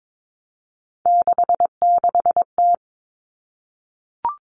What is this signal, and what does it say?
Morse code, heard as a keyed tone, about 22 wpm, 700 Hz. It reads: 66T
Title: Common Cut Numbers - 3-Digit 22wpm